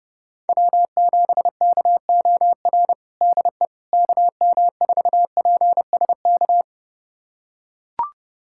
Morse code, heard as a keyed tone, about 30 wpm, 700 Hz. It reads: W7KOR DE KM4PSK
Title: USA Callsign DE Callsign 30wpm